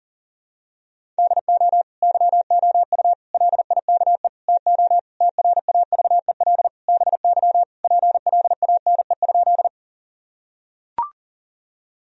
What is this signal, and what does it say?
Morse code, heard as a keyed tone, about 40 wpm, 700 Hz. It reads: DO YOU LIKE TO TRAVEL BY PLANE?
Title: Sentences from Top 1000 Words - Review Part 3 - 40wpm